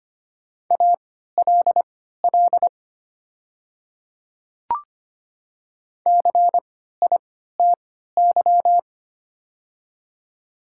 Morse code, heard as a keyed tone, about 25 wpm, spaced as for 14 wpm, 700 Hz. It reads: ALL CITY
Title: 3x Character Spacing - Top 300 Words - 25wpm